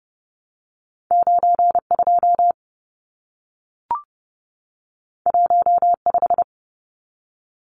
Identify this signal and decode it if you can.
Morse code, heard as a keyed tone, about 30 wpm, 700 Hz. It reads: 92 15
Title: Numbers - 2-Digit 30wpm